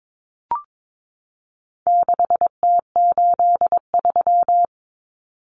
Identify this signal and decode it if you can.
Morse code, heard as a keyed tone, about 22 wpm, 700 Hz. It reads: 6T83
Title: Common Cut Numbers - 4-Digit 22wpm